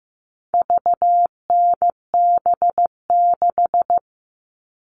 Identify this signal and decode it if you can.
Morse code, heard as a keyed tone, about 15 wpm, 700 Hz. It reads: VNB6
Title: All Cut Numbers - 4-Digit 15wpm